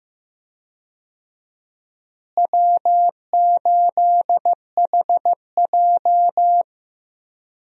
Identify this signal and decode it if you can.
Morse code, heard as a keyed tone, about 15 wpm, 700 Hz. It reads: W8HJ
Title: USA Call Signs - 4 Characters - Repeated 2x - 15wpm